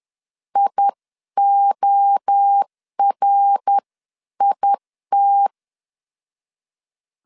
Morse code, heard as a keyed tone, about 10 wpm, 800 Hz. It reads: IORIT
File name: 10wpm